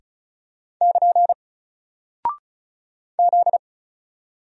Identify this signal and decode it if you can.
Morse code, heard as a keyed tone, about 36 wpm, 700 Hz. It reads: <KN> Z
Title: Single Character (Letter:Number:Punctuation:Prosign) Alt 35wpm